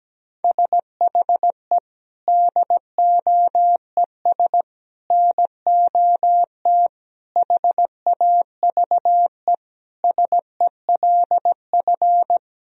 Morse code, heard as a keyed tone, about 17 wpm, 700 Hz. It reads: SHE DOES NOT HAVE SELF
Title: Sentences from Top 1000 Words - Review Part 3 - 17wpm